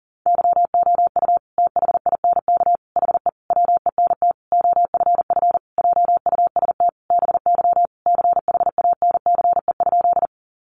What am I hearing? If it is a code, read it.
Morse code, heard as a keyed tone, about 40 wpm, 700 Hz. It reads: YOU THINK HE WENT OFF JUST BY CHANCE?
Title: Sentences from Top 800 Words 40wpm